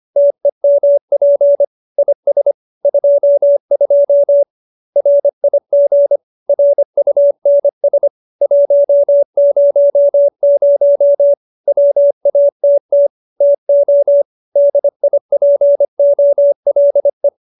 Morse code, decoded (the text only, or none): TEMP IS 22 RIG RUNS 100 WATT TO DIPOLE